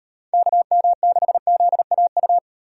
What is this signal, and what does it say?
Morse code, heard as a keyed tone, about 35 wpm, 700 Hz. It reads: KM6ZAU